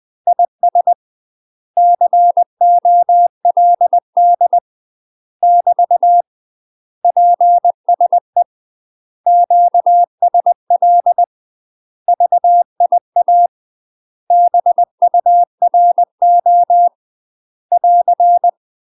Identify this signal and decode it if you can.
Morse code, heard as a keyed tone, about 20 wpm, 700 Hz. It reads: IS COLD <BT> PSE QSL VIA BURO <AR>
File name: UK MASTER QSOs TRAINING / 20WPM / UK MASTER QSOs 0002